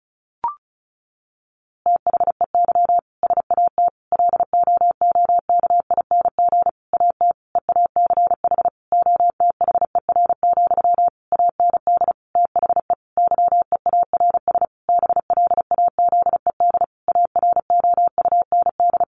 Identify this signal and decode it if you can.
Morse code, heard as a keyed tone, about 34 wpm, 700 Hz. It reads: THEY SAT LOOKING AT EACH OTHER, AND THE YEARS BLAZED AROUND
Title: A Sound Of Thunder 35wpm